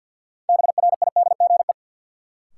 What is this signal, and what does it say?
Morse code, heard as a keyed tone, about 53 wpm, 700 Hz. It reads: BRIDGE
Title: Sentences from Top 1200 Words - New Word Drill - 50wpm